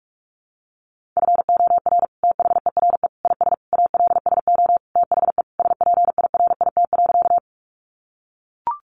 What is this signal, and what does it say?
Morse code, heard as a keyed tone, about 44 wpm, 700 Hz. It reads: FOR THERE IS ALSO THE SPIRIT.
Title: Sentences from Top 1400 Words 45wpm